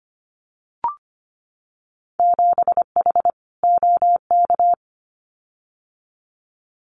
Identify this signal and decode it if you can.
Morse code, heard as a keyed tone, about 25 wpm, 700 Hz. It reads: 7H OK
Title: Winter Field Day Exchanges Encore 25wpm